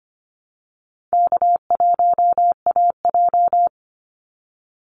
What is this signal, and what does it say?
Morse code, heard as a keyed tone, about 25 wpm, 700 Hz. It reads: K1AJ
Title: Call Signs Repeated 3x - 25wpm